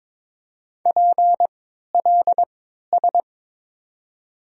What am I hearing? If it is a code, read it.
Morse code, heard as a keyed tone, about 22 wpm, spaced as for 13 wpm, 700 Hz. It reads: PLS
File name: Top 90 QSO Elements - 3x Character Spacing - 22wpm